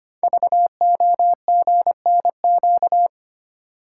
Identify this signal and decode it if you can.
Morse code, heard as a keyed tone, about 25 wpm, 700 Hz. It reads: VOGNQ
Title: Code Groups - 5 Letters - 25wpm